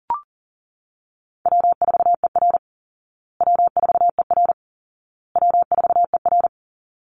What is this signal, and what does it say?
Morse code, heard as a keyed tone, about 37 wpm, 700 Hz. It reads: W4ER W4ER W4ER
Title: Easy Call Signs Repeated 3x - 40wpm